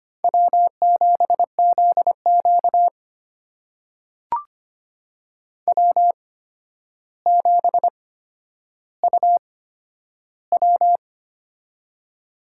Morse code, heard as a keyed tone, about 25 wpm, spaced as for 11 wpm, 700 Hz. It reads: W7ZQ W7UW
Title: USA Call Signs - 4 Characters - 8x Character Spacing - 25wpm